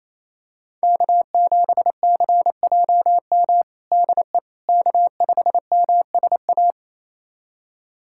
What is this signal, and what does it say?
Morse code, heard as a keyed tone, about 27 wpm, 700 Hz. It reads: K7CJM DE K5MSA
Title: USA Callsign DE Callsign - 5 Characters - 28wpm